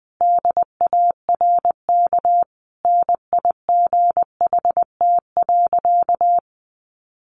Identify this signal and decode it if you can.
Morse code, heard as a keyed tone, about 20 wpm, 700 Hz. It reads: DARK NIGHT.
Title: Sentences from Top 500 Words 20wpm